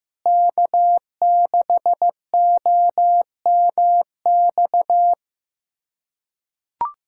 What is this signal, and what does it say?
Morse code, heard as a keyed tone, about 15 wpm, 700 Hz. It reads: K6OMX
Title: USA Callsign DE Callsign - 5 Characters - 15wpm